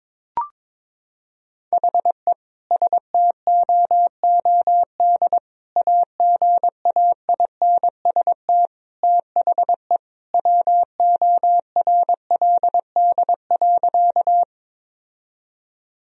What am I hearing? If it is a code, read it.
Morse code, heard as a keyed tone, about 22 wpm, 700 Hz. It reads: HE STOOD AGAINST THE WORLD.
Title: Sentences from Top 1000 Words - Review Part 2 - 22wpm